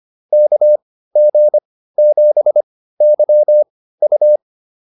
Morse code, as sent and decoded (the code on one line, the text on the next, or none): -.- --. --... -.-- ..-
KG7YU